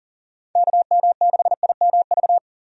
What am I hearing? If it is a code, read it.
Morse code, heard as a keyed tone, about 37 wpm, 700 Hz. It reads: KM6IMV
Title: USA Call Signs - 6 Characters - 40wpm